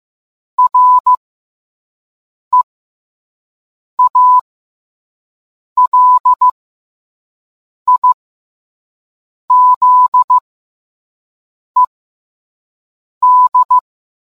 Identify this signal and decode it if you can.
Morse code, heard as a keyed tone, about 15 wpm, spaced as for 5 wpm, 1000 Hz. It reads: REALIZED